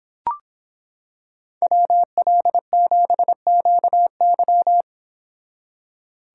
Morse code, heard as a keyed tone, about 25 wpm, 700 Hz. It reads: WL7QY